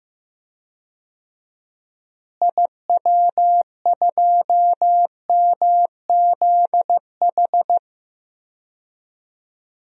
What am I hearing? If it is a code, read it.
Morse code, heard as a keyed tone, about 15 wpm, 700 Hz. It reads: IW2MZH
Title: DXCC-100 Call Signs - 6 Characters - Repeated 3x - 15wpm